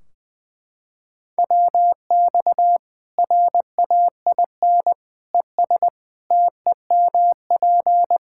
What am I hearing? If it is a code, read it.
Morse code, heard as a keyed tone, about 21 wpm, 700 Hz. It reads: WX RAIN ES TEMP
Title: Practice QSO - Exchange 2 - Info - 20wpm